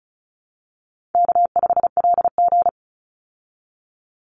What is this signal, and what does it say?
Morse code, heard as a keyed tone, about 34 wpm, 700 Hz. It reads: K5LG
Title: Call Signs Repeated 3x - 35wpm